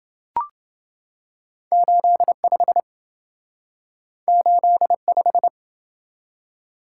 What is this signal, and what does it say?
Morse code, heard as a keyed tone, about 27 wpm, 700 Hz. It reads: 85 85